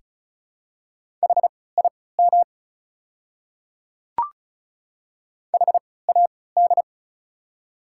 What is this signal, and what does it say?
Morse code, heard as a keyed tone, about 36 wpm, spaced as for 20 wpm, 690 Hz. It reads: HIM HAD
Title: 3x Character Spacing - Top 100 Words - 35wpm